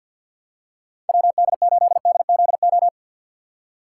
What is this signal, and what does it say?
Morse code, heard as a keyed tone, about 47 wpm, 700 Hz. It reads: W D 8 D Z O
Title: USA Call Signs - 6 Characters - 6x Character Spacing - 50wpm